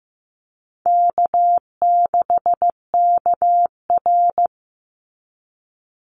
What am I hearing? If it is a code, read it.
Morse code, heard as a keyed tone, about 15 wpm, 700 Hz. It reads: K6KR
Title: Easy Call Signs Repeated 2x - 15wpm